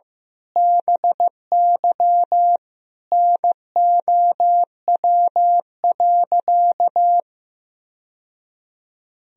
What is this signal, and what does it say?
Morse code, heard as a keyed tone, about 15 wpm, 700 Hz. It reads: BY NOW.
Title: Sentences from Top 1000 Words - Review Part 3 - 15wpm